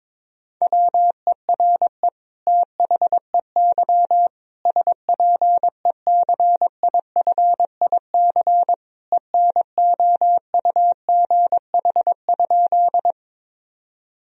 Morse code, heard as a keyed tone, about 22 wpm, 700 Hz. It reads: WERE THEY SPECIFIC ENOUGH?
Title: Sentences from Top 1000 Words - Review Part 3 - 22wpm